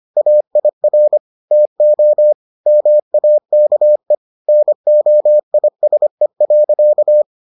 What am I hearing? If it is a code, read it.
Morse code, heard as a keyed tone, about 25 wpm, 600 Hz. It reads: AIR TO MAKE NOISE.